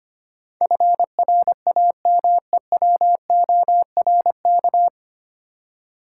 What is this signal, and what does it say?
Morse code, heard as a keyed tone, about 25 wpm, 700 Hz. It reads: FRAMEWORK
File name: Nine Letter Words - 25wpm